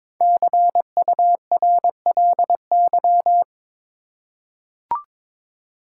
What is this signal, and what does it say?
Morse code, heard as a keyed tone, about 22 wpm, 700 Hz. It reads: CURLY
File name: practice / 22wpm / 5 Letter Words - First 3 Letters Given - 22wpm